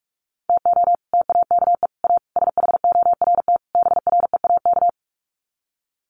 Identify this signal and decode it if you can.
Morse code, heard as a keyed tone, about 44 wpm, 690 Hz. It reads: TO TAKE A SHORT BREAK
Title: Sets of 5 Words Encore2 45wpm